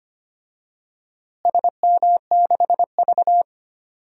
Morse code, heard as a keyed tone, about 25 wpm, 700 Hz. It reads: SM6V
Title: DXCC-100 Call Signs - 4 Characters - Repeated 3x - 25wpm